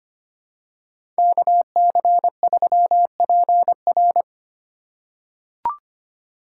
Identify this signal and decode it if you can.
Morse code, heard as a keyed tone, about 25 wpm, 700 Hz. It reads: KC3PR
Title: USA Call Signs - 3x Character Spacing - 25wpm